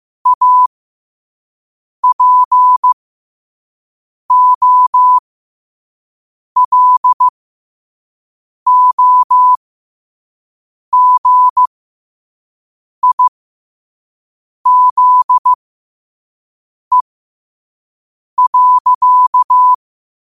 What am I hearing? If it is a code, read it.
Morse code, heard as a keyed tone, about 15 wpm, spaced as for 5 wpm, 1000 Hz. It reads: APOLOGIZE.
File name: qotdfiles / 5wpm